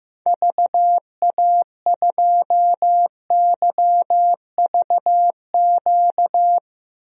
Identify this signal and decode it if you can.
Morse code, heard as a keyed tone, about 16 wpm, 700 Hz. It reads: VA2YVQ
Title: DXCC-100 Call Signs - 6 Characters - 15wpm